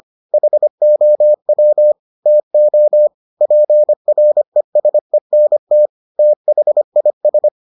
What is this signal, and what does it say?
Morse code, heard as a keyed tone, about 25 wpm, 600 Hz. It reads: HOW TO PRESENT THIS